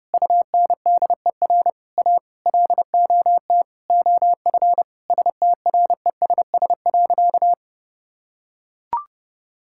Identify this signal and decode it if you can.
Morse code, heard as a keyed tone, about 30 wpm, 700 Hz. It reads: UNDER A LOT OF STRESS.
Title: Sentences from Top 1000 Words - Review Part 6 - 30wpm